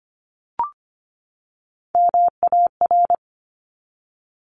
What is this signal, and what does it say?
Morse code, heard as a keyed tone, about 25 wpm, 700 Hz. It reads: MAR
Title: ARRL Sections 25wpm